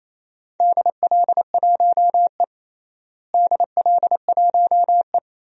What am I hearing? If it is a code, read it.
Morse code, heard as a keyed tone, about 28 wpm, 700 Hz. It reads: DL1E DL1E
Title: DXCC-100 Call Signs - 4 Characters - Repeated 2x - 28wpm